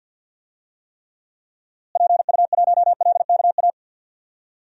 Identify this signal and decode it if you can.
Morse code, heard as a keyed tone, about 50 wpm, 700 Hz. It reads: PU1LKA
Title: DXCC-100 Call Signs - 6 Characters - Repeated 3x - 50wpm